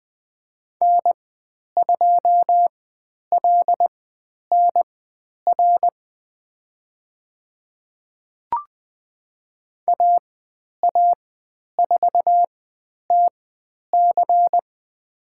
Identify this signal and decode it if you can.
Morse code, heard as a keyed tone, about 20 wpm, spaced as for 10 wpm, 700 Hz. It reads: N2LNR AA4TC
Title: 109 Lesson 18g - Letter m - Call Signs 20wpm